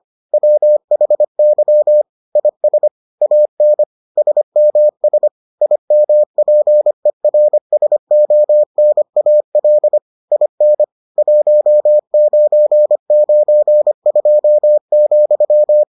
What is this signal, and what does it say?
Morse code, heard as a keyed tone, about 25 wpm, 600 Hz. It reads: WHY IS AN SMS IMPERSONAL IN 1992,